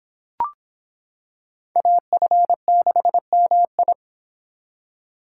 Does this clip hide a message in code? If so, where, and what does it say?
Yes: Morse code from 1.8 to 3.9 s, heard as a keyed tone, about 25 wpm, 700 Hz. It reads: AF6MI